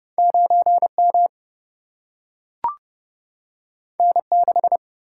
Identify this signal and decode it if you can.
Morse code, heard as a keyed tone, about 30 wpm, 700 Hz. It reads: 9M N6
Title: Single-Letter Single-Number 30wpm